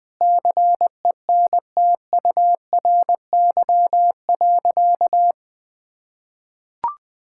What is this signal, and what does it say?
Morse code, heard as a keyed tone, about 21 wpm, 700 Hz. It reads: CENTURY.
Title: Sentences with 5-Word Phrases - Spoken Beginning - 20wpm